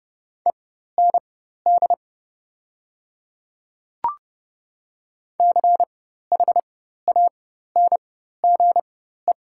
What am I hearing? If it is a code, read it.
Morse code, heard as a keyed tone, about 30 wpm, spaced as for 14 wpm, 700 Hz. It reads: END CHANGE
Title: 4x Character Spacing - Top 200 Words - 30wpm